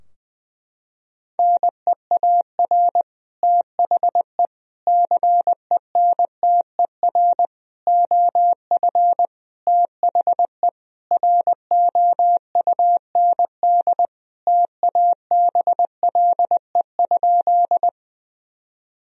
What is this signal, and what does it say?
Morse code, heard as a keyed tone, about 20 wpm, 700 Hz. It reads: NEAR THE CENTER OF THE ROUND TABLE?
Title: Sentences from Top 1000 Words - Review Part 2 - 20wpm